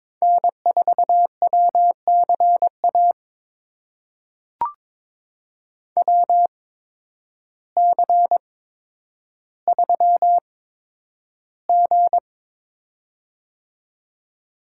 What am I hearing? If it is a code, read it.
Morse code, heard as a keyed tone, about 22 wpm, spaced as for 10 wpm, 700 Hz. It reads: N4WCA WC3G
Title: USA Call Signs - 8x Character Spacing - 22wpm